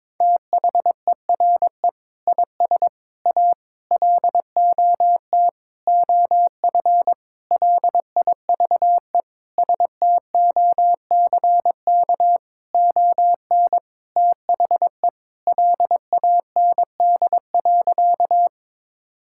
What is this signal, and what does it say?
Morse code, heard as a keyed tone, about 22 wpm, 700 Hz. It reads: THERE IS A LOT OF LIVESTOCK ON THE LAND.
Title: Sentences from Top 1000 Words - Review Part 5 - 22wpm